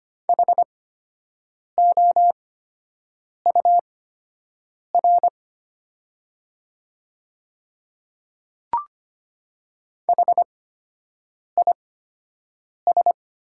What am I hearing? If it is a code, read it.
Morse code, heard as a keyed tone, about 25 wpm, spaced as for 7 wpm, 700 Hz. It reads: HOUR HIS
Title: 8x Character Spacing - Top 500 Words - 25wpm